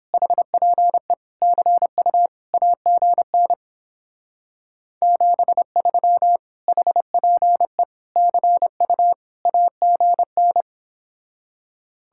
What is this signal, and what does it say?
Morse code, heard as a keyed tone, about 29 wpm, 700 Hz. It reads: HPE CU AGN 73 HPE CU AGN